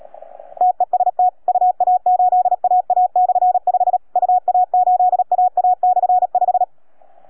Morse code, heard as a keyed tone, about 37 wpm, 730 Hz. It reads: TEST UA8AA/5 UA8AA/5